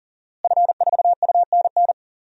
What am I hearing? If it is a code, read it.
Morse code, heard as a keyed tone, about 40 wpm, 700 Hz. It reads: F4UNN
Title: DXCC-100 Call Signs - 5 Characters - Repeated 2x - 40wpm